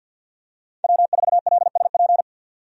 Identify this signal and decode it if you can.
Morse code, heard as a keyed tone, about 50 wpm, 700 Hz. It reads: W4LSP
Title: USA Call Signs - 5 Characters - 50wpm